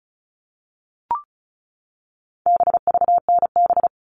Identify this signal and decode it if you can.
Morse code, heard as a keyed tone, about 34 wpm, 700 Hz. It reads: BVNB
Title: All Cut Numbers - 4-Digit 35wpm